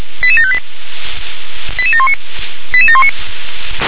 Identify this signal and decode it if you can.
Five-tone selective call (ZVEI), heard as quick runs of five tones, tones of about 70 ms. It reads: 88758 88719 88719
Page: ZVEI